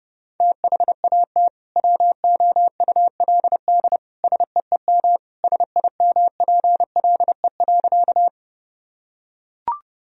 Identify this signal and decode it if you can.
Morse code, heard as a keyed tone, about 31 wpm, 700 Hz. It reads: THAT WOULD SEEM SIMPLE.
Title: Sentences from Top 1000 Words - Review Part 1 - 30wpm